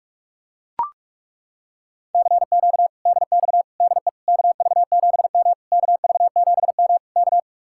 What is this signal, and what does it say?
Morse code, heard as a keyed tone, about 40 wpm, 700 Hz. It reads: CQ DX DE KV7M KV7M K